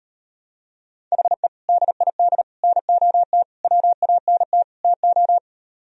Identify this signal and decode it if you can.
Morse code, heard as a keyed tone, about 38 wpm, 700 Hz. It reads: HE DID NOT WANT TO